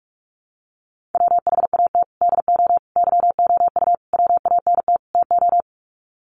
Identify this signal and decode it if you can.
Morse code, heard as a keyed tone, about 44 wpm, 700 Hz. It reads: WHAT DO YOU WANT TO
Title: Sets of 5 Words 45wpm